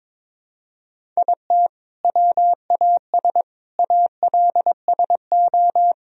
Morse code, heard as a keyed tone, about 22 wpm, 700 Hz. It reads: IT WAS ALSO
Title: Sentences from Top 1000 Words - Review Part 1 - 22wpm